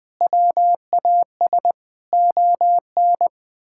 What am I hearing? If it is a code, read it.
Morse code, heard as a keyed tone, about 21 wpm, 700 Hz. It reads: WAS ON
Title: Sentences with 4-Word Phrases - 20wpm